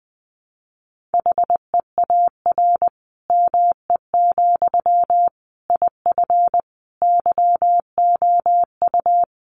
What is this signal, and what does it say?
Morse code, heard as a keyed tone, about 20 wpm, 700 Hz. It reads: HEAR ME, IF YOU
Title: Sentences from Top 1500 Words 20wpm